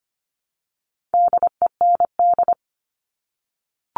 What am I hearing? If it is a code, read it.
Morse code, heard as a keyed tone, about 25 wpm, 700 Hz. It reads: DEND
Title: All Cut Numbers - 4-Digit 25wpm